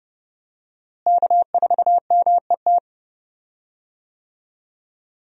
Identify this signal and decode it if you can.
Morse code, heard as a keyed tone, about 30 wpm, 700 Hz. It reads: K4MET
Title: USA Call Signs - 5 Characters - Repeated 2x - 30wpm